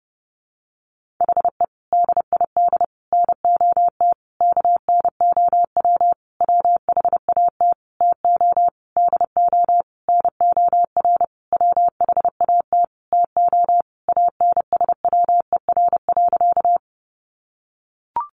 Morse code, heard as a keyed tone, about 30 wpm, 700 Hz. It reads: HE DID NOT KNOW WHAT TO DO NOR WHAT TO ANSWER.
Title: Sentences from Top 1500 Words 30wpm